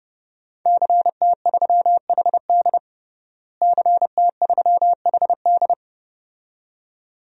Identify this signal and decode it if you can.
Morse code, heard as a keyed tone, about 30 wpm, 700 Hz. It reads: CT3HD CT3HD
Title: DXCC-100 Call Signs - 5 Characters - Repeated 2x - 30wpm